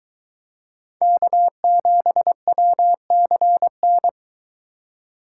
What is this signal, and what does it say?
Morse code, heard as a keyed tone, about 23 wpm, 700 Hz. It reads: K7WCN